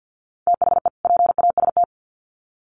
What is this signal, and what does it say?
Morse code, heard as a keyed tone, about 50 wpm, 700 Hz. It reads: THE PAST
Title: Sets of 2 Words 50wpm